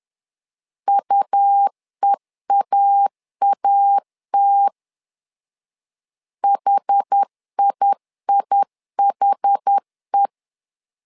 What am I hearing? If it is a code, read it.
Morse code, heard as a keyed tone, about 10 wpm, 800 Hz. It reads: UEAAT HIIHE
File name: LEZIONI / 2 UHDN / 2 UH 10wpm 1714